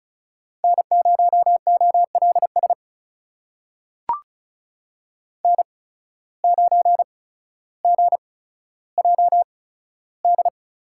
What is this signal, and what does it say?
Morse code, heard as a keyed tone, about 34 wpm, spaced as for 15 wpm, 700 Hz. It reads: N0OLS N9GJD
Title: USA Call Signs - 5 Characters - 8x Character Spacing - 35wpm